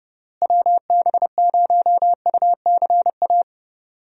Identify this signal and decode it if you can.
Morse code, heard as a keyed tone, about 30 wpm, 700 Hz. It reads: WB0UCA
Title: USA Call Signs - 6 Characters - Repeated 2x - 30wpm